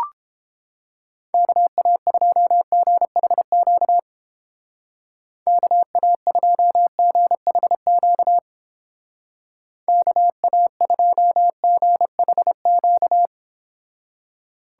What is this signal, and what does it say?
Morse code, heard as a keyed tone, about 31 wpm, 700 Hz. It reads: KA2GHQ KA2GHQ KA2GHQ